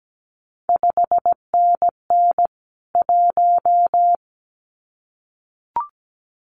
Morse code, heard as a keyed tone, about 17 wpm, 700 Hz. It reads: 5NN 1
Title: CQ WW DX Contest Exchange Encore - 17wpm